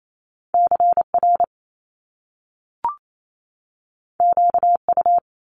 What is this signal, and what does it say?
Morse code, heard as a keyed tone, about 28 wpm, 700 Hz. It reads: CR QU
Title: Top Two Letter Combinations in Words 28wpm